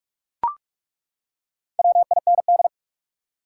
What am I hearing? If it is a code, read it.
Morse code, heard as a keyed tone, about 41 wpm, 700 Hz. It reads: WIND
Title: Four Letter Words - 45wpm